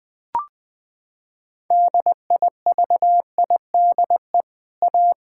Morse code, heard as a keyed tone, about 20 wpm, 700 Hz. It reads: DIVIDE A
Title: Sentences from Top 1600 Words 20wpm